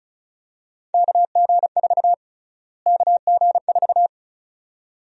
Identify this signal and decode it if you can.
Morse code, heard as a keyed tone, about 36 wpm, 700 Hz. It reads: KG4 KG4
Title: USA Call Sign Prefixes - Repeated 2x - 35wpm